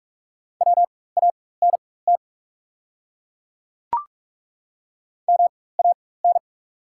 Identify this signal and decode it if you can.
Morse code, heard as a keyed tone, about 43 wpm, spaced as for 21 wpm, 690 Hz. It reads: WANT MAN
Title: 4x Character Spacing - Top 200 Words - 45wpm